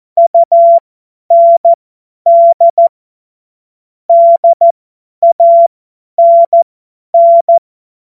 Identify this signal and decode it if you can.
Morse code, heard as a keyed tone, about 14 wpm, spaced as for 10 wpm, 680 Hz. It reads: UND DANN
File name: Morsekurs / Witze / Witz 02 April 2024 50 F70